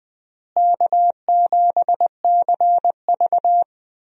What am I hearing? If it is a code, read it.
Morse code, heard as a keyed tone, about 20 wpm, 700 Hz. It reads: K7CV
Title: USA Call Signs - 4 Characters - Repeated 3x - 20wpm